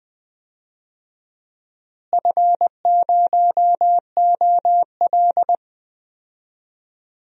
Morse code, heard as a keyed tone, about 20 wpm, 700 Hz. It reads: F0OL
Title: DXCC-100 Call Signs - 4 Characters - Repeated 3x - 20wpm